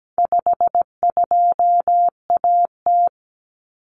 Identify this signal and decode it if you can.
Morse code, heard as a keyed tone, about 17 wpm, 700 Hz. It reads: 52AT
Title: Common Cut Numbers - 4-Digit 17wpm